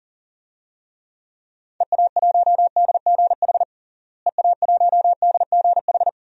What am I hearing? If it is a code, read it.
Morse code, heard as a keyed tone, about 37 wpm, 700 Hz. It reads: EA1DGH EA1DGH
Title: DXCC-100 Call Signs - 6 Characters - Repeated 3x - 40wpm